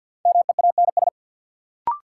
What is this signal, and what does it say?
Morse code, heard as a keyed tone, about 52 wpm, 700 Hz. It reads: MEANS
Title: Sets of 5 Words - New Word Drill - 50wpm